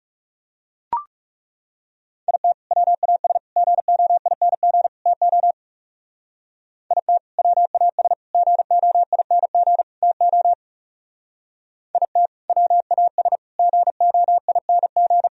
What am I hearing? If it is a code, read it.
Morse code, heard as a keyed tone, about 36 wpm, 700 Hz. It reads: IT WAS GOING TO IT WAS GOING TO IT WAS GOING
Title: Speed-Racing - Sets of 4 Words 30wpm